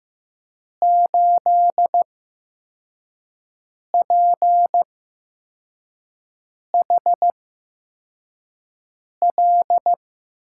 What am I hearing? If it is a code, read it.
Morse code, heard as a keyed tone, about 15 wpm, spaced as for 4 wpm, 700 Hz. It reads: 8PHL
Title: USA Call Signs - 5 Characters - 8x Character Spacing - 15wpm